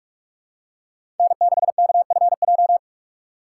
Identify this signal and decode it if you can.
Morse code, heard as a keyed tone, about 47 wpm, 690 Hz. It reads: N6KFJ
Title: USA Call Signs - 5 Characters - 2x Character Spacing - 45wpm